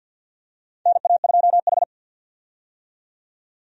Morse code, heard as a keyed tone, about 50 wpm, 700 Hz. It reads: NA2H
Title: USA Call Signs - 4 Characters - 50wpm